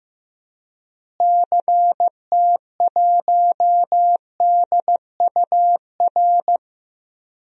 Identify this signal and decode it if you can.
Morse code, heard as a keyed tone, about 15 wpm, 700 Hz. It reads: CT1DUR
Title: DXCC-100 Call Signs - 6 Characters - Repeated 3x - 15wpm